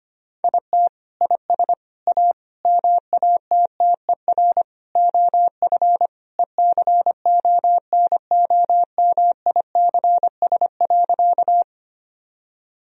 Morse code, heard as a keyed tone, about 25 wpm, 700 Hz. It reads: IT IS A MATTER OF ECONOMICS.
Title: Sentences from Top 1000 Words - Review Part 3 - 25wpm